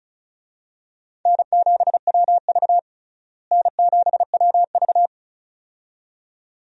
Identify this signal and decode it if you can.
Morse code, heard as a keyed tone, about 36 wpm, 700 Hz. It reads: N7WV N7WV
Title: USA Call Signs - 4 Characters - Repeated 2x - 35wpm